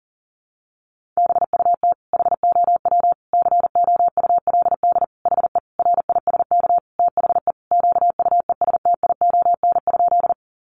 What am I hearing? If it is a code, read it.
Morse code, heard as a keyed tone, about 40 wpm, 700 Hz. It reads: BUT HOW COULD HE RISK THE QUESTION?
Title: Sentences from Top 600 Words 40wpm